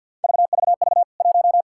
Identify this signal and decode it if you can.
Morse code, heard as a keyed tone, about 48 wpm, 700 Hz. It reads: VVV 1
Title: K1USN SST QSO Practice 50wpm